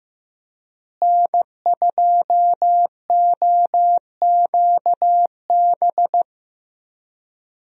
Morse code, heard as a keyed tone, about 15 wpm, 700 Hz. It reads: N2OQB
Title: USA Call Signs - 5 Characters - Repeated 3x - 15wpm